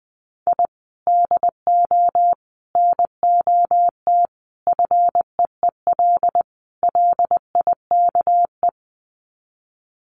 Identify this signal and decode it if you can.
Morse code, heard as a keyed tone, about 20 wpm, 700 Hz. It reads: I DO NOT FEEL LIKE
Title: Sets of 5 Words Encore 20wpm